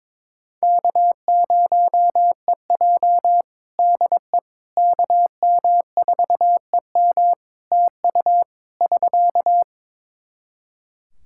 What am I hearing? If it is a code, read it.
Morse code, heard as a keyed tone, about 22 wpm, 700 Hz. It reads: K0EJ DE KM4EM TU <SK>
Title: Practice QSO - Exchange 4 - Ending - 22wpm